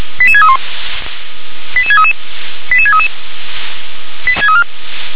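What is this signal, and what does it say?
Five-tone selective call (ZVEI), heard as quick runs of five tones, tones of about 70 ms. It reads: 88631 88633 88633
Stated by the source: ZVEI five-tone selective call received via SDR.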